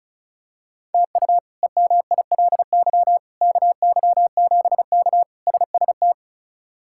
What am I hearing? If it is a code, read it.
Morse code, heard as a keyed tone, about 36 wpm, 700 Hz. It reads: TU EMILY KY7K SST
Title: K1USN SST QSO Practice - 35wpm